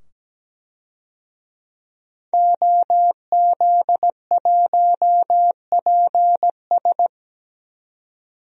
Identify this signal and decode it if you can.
Morse code, heard as a keyed tone, about 17 wpm, 700 Hz. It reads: OZ1PS
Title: DXCC-100 Call Signs - 5 Characters - 17wpm